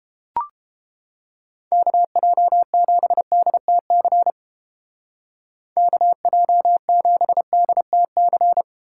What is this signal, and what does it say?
Morse code, heard as a keyed tone, about 31 wpm, 700 Hz. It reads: KJ7DTC KJ7DTC